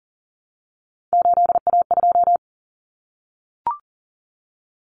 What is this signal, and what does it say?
Morse code, heard as a keyed tone, about 40 wpm, 700 Hz. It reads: 8A2
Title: Common Cut Numbers - 3-Digit 40wpm